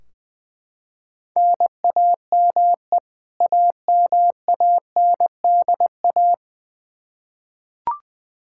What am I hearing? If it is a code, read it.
Morse code, heard as a keyed tone, about 20 wpm, 700 Hz. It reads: NAME AMANDA
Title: Practice QSO - Exchange 1 - Beginning - 20wpm